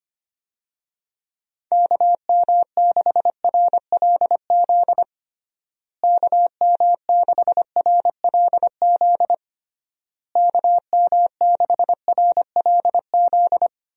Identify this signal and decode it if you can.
Morse code, heard as a keyed tone, about 25 wpm, 700 Hz. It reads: KM6RLZ KM6RLZ KM6RLZ
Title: USA Call Signs - 6 Characters - Repeated 3x - 25wpm